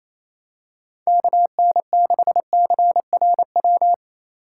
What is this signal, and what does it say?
Morse code, heard as a keyed tone, about 28 wpm, 700 Hz. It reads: KN6CRW
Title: USA Call Signs - 2x Character Spacing - 28wpm